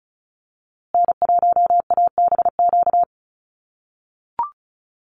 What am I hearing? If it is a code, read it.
Morse code, heard as a keyed tone, about 34 wpm, 700 Hz. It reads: N1ABQ
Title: USA Call Signs Repeated 3x 35wpm